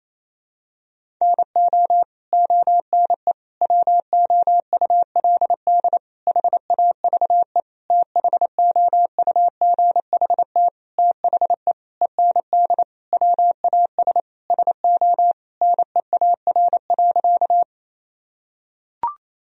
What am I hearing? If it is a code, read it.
Morse code, heard as a keyed tone, about 28 wpm, 700 Hz. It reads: NO ONE WOULD HAVE THOUGHT THE END WAS SO NEAR.
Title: Sentences from Top 1000 Words - Review Part 1 - 28wpm